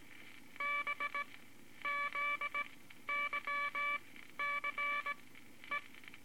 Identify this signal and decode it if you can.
Morse code, heard as a keyed tone, about 17 wpm, spaced as for 11 wpm, 2300 Hz, with noise. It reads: BZYCE
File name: Morse Code Records / 18 B LCM 165 Ten Words Per Minute